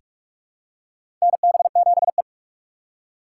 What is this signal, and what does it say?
Morse code, heard as a keyed tone, about 42 wpm, 700 Hz. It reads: NB7E
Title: USA Call Signs - 4 Characters - 3x Character Spacing - 45wpm